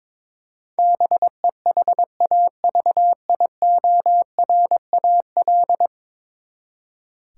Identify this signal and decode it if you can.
Morse code, heard as a keyed tone, about 22 wpm, 700 Hz. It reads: BEHAVIORAL
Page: morse-code-ninja-practice-22wpm